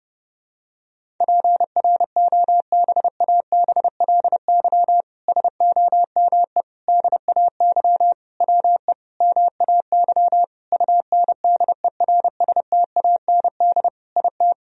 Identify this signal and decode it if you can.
Morse code, heard as a keyed tone, about 30 wpm, 700 Hz. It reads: PROBABLY SOME DAY WE MAY UNDERSTAND IT
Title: Sentences from Top 1000 Words - Review Part 4 - 30wpm